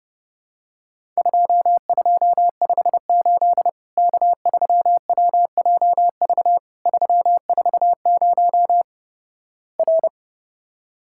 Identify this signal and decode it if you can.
Morse code, heard as a keyed tone, about 31 wpm, 700 Hz. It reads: 2258 K3WJV 340 R
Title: Worked All Europe DX Contest - QTC Exchange - 30wpm